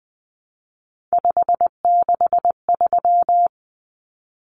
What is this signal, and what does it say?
Morse code, heard as a keyed tone, about 20 wpm, 700 Hz. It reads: 563
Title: Common Cut Numbers - 3-Digit 20wpm